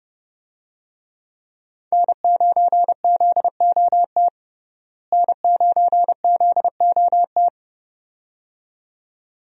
Morse code, heard as a keyed tone, about 32 wpm, 700 Hz. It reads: N9ZOT N9ZOT
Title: USA Call Signs - 5 Characters - Repeated 2x - 30wpm